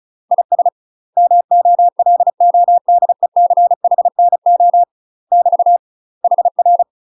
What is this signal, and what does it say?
Morse code, heard as a keyed tone, about 34 wpm, 700 Hz. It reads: IS MOLODECHNO <BT> HR
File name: UK MASTER QSOs TRAINING / 35WPM / UK MASTER QSOs 0042